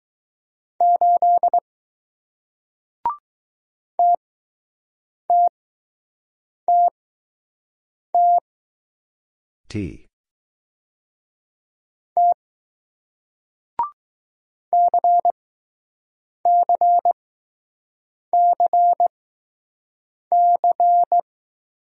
Morse code, heard as a keyed tone, about 20 wpm, 700 Hz. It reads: 8 T T T T T C C C C